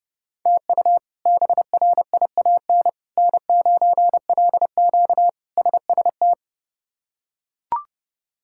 Morse code, heard as a keyed tone, about 32 wpm, 700 Hz. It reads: TU BRIAN N9LQ SST
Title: K1USN SST QSO Practice - 30wpm